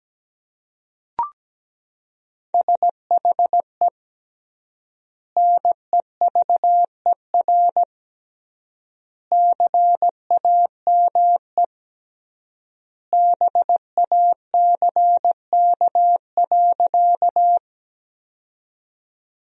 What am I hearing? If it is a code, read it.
Morse code, heard as a keyed tone, about 17 wpm, 700 Hz. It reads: SHE NEVER CAME BACK.
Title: 3x Word-Spacing - Sentences from Top 300 Words - 17wpm